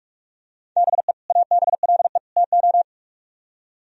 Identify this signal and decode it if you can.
Morse code, heard as a keyed tone, about 41 wpm, 700 Hz. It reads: BE ABLE TO